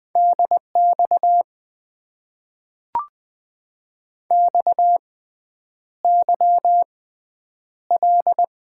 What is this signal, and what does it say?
Morse code, heard as a keyed tone, about 20 wpm, spaced as for 9 wpm, 700 Hz. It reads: DX XYL
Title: Top 90 QSO Elements - 6x Character Spacing - 20wpm